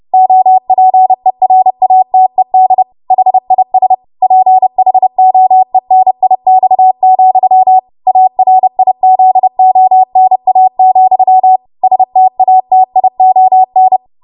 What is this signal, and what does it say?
Morse code, heard as a keyed tone, about 30 wpm, 750 Hz. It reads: OPERATED HIS PHOENIX, ARIZONA, STATION
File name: Archive / 30 WPM / 30WPM